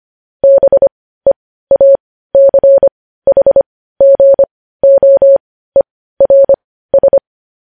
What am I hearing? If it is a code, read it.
Morse code, heard as a keyed tone, about 25 wpm, spaced as for 15 wpm, 550 Hz. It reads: BEACHGOERS
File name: podcast / News headlines-1x-25x15